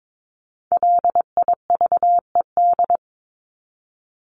Morse code, heard as a keyed tone, about 22 wpm, 700 Hz. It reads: LIVED
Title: ED Suffix - Top 100 Words with Base Words 22wpm